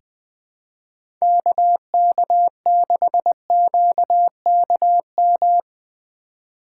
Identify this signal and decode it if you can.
Morse code, heard as a keyed tone, about 20 wpm, 700 Hz. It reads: KK6QKM
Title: USA Call Signs - 3x Character Spacing - 20wpm